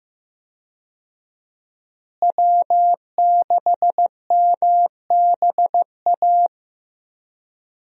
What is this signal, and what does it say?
Morse code, heard as a keyed tone, about 16 wpm, 700 Hz. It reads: W6MBA
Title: USA Call Signs - 5 Characters - Repeated 2x - 15wpm